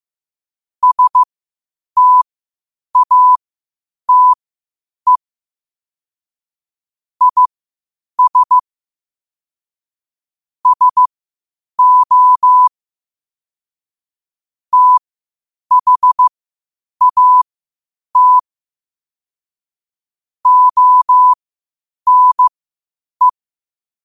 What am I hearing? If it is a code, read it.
Morse code, heard as a keyed tone, about 15 wpm, spaced as for 8 wpm, 1000 Hz. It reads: STATE IS SO THAT ONE
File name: qotdfiles / 8wpm